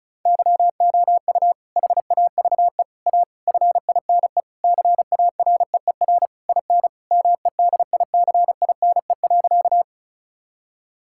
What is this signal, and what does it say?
Morse code, heard as a keyed tone, about 36 wpm, 700 Hz. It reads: YOU HAVE A FINE CAREER IN MEDICINE.
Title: Sentences from Top 1000 Words - Review Part 6 - 35wpm